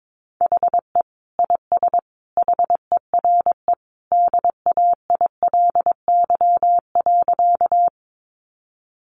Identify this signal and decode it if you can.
Morse code, heard as a keyed tone, about 22 wpm, 700 Hz. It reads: HE IS HERE DAILY.
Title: Sentences from Top 1100 Words 22wpm